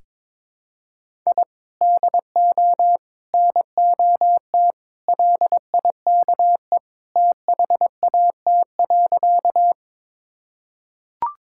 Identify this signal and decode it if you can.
Morse code, heard as a keyed tone, about 22 wpm, 700 Hz. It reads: I DO NOT LIKE THAT.
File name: practice / 22wpm / Sentences with 4-Word Phrases - 22wpm